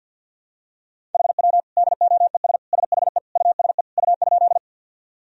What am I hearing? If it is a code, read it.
Morse code, heard as a keyed tone, about 51 wpm, 700 Hz. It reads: HW DOES SHE USE U?
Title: Sentences from Top 200 Words and Abbreviations 50wpm